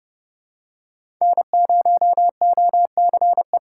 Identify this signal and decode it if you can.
Morse code, heard as a keyed tone, about 30 wpm, 700 Hz. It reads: N0OCE
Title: USA Call Signs - 6x Character Spacing - 30wpm